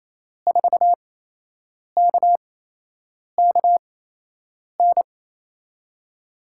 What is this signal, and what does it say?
Morse code, heard as a keyed tone, about 28 wpm, spaced as for 8 wpm, 700 Hz. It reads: 4KKN
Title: USA Call Signs - 6 Characters - 8x Character Spacing - 28wpm